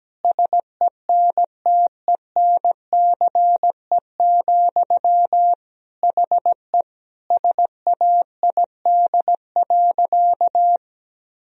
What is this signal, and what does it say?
Morse code, heard as a keyed tone, about 17 wpm, 700 Hz. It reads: SENTENCE, HE SAID.
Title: Sentences from Top 1000 Words - Review Part 1 - 17wpm